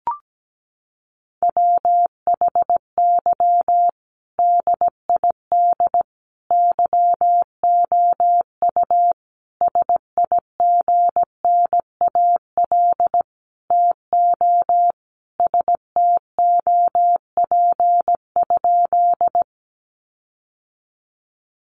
Morse code, heard as a keyed tone, about 17 wpm, 700 Hz. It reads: WHY DID YOU SIGNAL TO STOP?
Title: Sentences from Top 1400 Words 17wpm